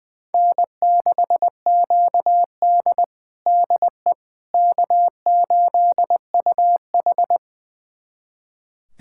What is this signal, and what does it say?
Morse code, heard as a keyed tone, about 20 wpm, 700 Hz. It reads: N6QD DE K8UH
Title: USA Callsign DE Callsign - 4 Characters - 20wpm